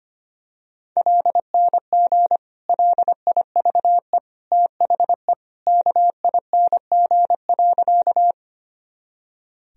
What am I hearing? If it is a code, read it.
Morse code, heard as a keyed tone, about 25 wpm, 700 Hz. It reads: LNG LIVE THE KING.
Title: Sentences from Top 400 Words and Abbreviations 25wpm